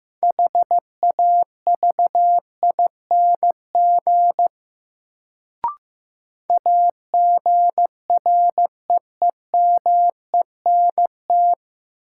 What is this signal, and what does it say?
Morse code, heard as a keyed tone, about 15 wpm, 700 Hz. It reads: HAVING AGREEMENT
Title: Sets of 5 Words Encore - New Word Drill - 15wpm